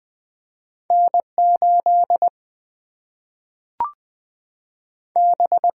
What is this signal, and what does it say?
Morse code, heard as a keyed tone, about 20 wpm, 700 Hz. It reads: N 8 B
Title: Single-Letter Single-Number 20wpm